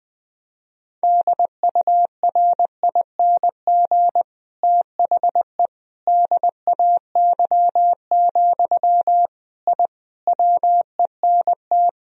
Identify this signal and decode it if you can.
Morse code, heard as a keyed tone, about 21 wpm, 700 Hz. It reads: DURING THE DAY, I WENT
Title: Sentences from Top 1000 Words - Review Part 2 - 20wpm